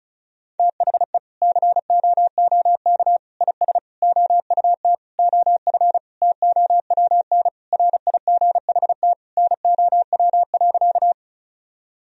Morse code, heard as a keyed tone, about 34 wpm, 700 Hz. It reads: THE COOK IS OUT OF TOWN RIGHT NOW.
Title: Sentences from Top 1000 Words - Review Part 4 - 35wpm